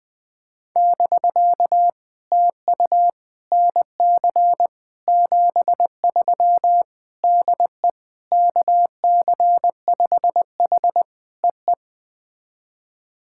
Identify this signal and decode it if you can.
Morse code, heard as a keyed tone, about 20 wpm, 700 Hz. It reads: <BK> TU NC 73 DE KC5H EE
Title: POTA QSO Practice - 20wpm